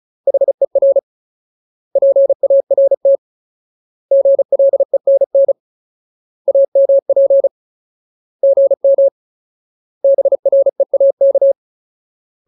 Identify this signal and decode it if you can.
Morse code, heard as a keyed tone, about 34 wpm, 550 Hz. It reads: HER PART GLENN AMP GM BREAK
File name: Ultimate Word List / 35WPM Ultimate